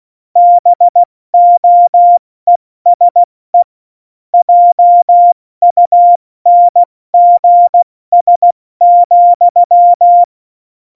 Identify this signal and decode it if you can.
Morse code, heard as a keyed tone, about 16 wpm, 700 Hz. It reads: BOESE JUNGS,
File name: Morsekurs / Witze / Witz 24. Oktober 17, Tempo 70